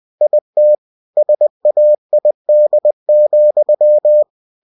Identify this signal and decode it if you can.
Morse code, heard as a keyed tone, about 20 wpm, 600 Hz. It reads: IT SAID,
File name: cw-books / winnie-the-pooh / winnie-20wpm-page-0052